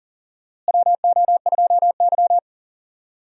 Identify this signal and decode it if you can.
Morse code, heard as a keyed tone, about 40 wpm, 700 Hz. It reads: WO2Y
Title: USA Call Signs - 4 Characters - Repeated 3x - 40wpm